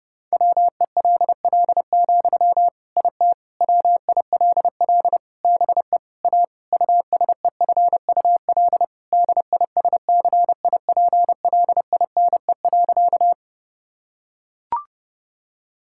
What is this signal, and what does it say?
Morse code, heard as a keyed tone, about 30 wpm, 700 Hz. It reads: WELL, IT WILL BE A USEFUL DISCIPLINE.
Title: Sentences from Top 1000 Words - Review Part 6 - 30wpm